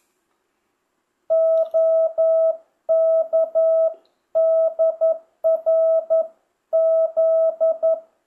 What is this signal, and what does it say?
Morse code, heard as a keyed tone, about 11 wpm, 650 Hz. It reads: OKDRZ
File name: jh8cbh / 2025cw